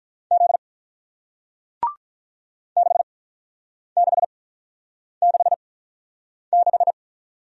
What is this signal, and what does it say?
Morse code, heard as a keyed tone, about 47 wpm, spaced as for 9 wpm, 700 Hz. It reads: Z 6666